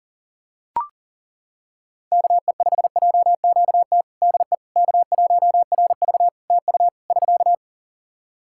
Morse code, heard as a keyed tone, about 40 wpm, 700 Hz. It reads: KE5JQT DE K1RV TU <SK>
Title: Practice QSO - Exchange 4 - Ending - 40wpm